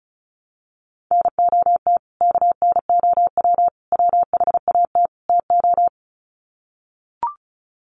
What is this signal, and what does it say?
Morse code, heard as a keyed tone, about 34 wpm, 700 Hz. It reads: NOT KNOW WHAT TO
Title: Sets of 4 Words 35wpm